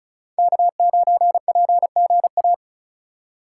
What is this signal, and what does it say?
Morse code, heard as a keyed tone, about 34 wpm, 700 Hz. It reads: K9PGA
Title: USA Call Signs - 5 Characters - 3x Character Spacing - 35wpm